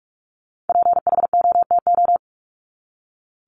Morse code, heard as a keyed tone, about 44 wpm, 700 Hz. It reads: PHOTO
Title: Top 1001-1250 Words 45wpm